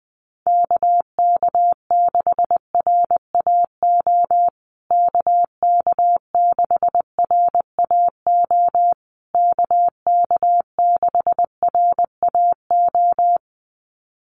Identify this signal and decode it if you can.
Morse code, heard as a keyed tone, about 20 wpm, 700 Hz. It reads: KK6RAO KK6RAO KK6RAO
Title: USA Call Signs Repeated 3x 20wpm